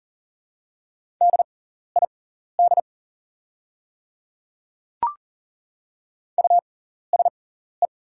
Morse code, heard as a keyed tone, about 39 wpm, spaced as for 14 wpm, 700 Hz. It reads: DID USE
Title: 6x Character Spacing - Top 100 Words - 40wpm